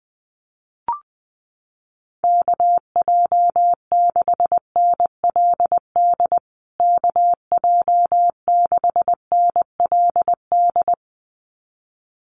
Morse code, heard as a keyed tone, about 20 wpm, 700 Hz. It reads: KJ6NLD KJ6NLD
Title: USA Call Signs Repeated 2x 20wpm